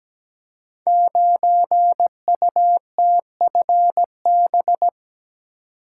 Morse code, heard as a keyed tone, about 17 wpm, 700 Hz. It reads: 9UTFB
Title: Code Groups - 5 Letter:Numbers - 17wpm